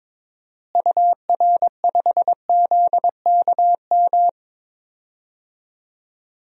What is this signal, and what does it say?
Morse code, heard as a keyed tone, about 22 wpm, 700 Hz. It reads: UR5ZKM
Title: DXCC-100 Call Signs - 6 Characters - Repeated 3x - 22wpm